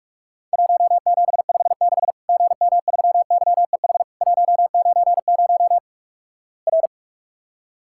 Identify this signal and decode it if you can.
Morse code, heard as a keyed tone, about 46 wpm, 700 Hz. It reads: 1756 GM3YEH 190 R
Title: Worked All Europe DX Contest - QTC Exchange - 45wpm